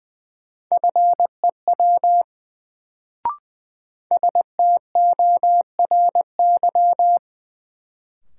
Morse code, heard as a keyed tone, about 20 wpm, 700 Hz. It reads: FEW STORY
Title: Sentences from Top 300 Words - New Word Drill - 20wpm